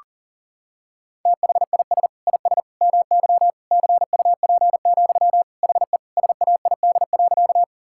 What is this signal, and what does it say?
Morse code, heard as a keyed tone, about 40 wpm, 700 Hz. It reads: THIS IS MY CUP, HE SAID.
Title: Sentences from Top 1600 Words 40wpm